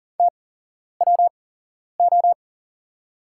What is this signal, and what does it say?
Morse code, heard as a keyed tone, about 42 wpm, spaced as for 11 wpm, 690 Hz. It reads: TWO
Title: 8x Character Spacing - Top 100 Words - 40wpm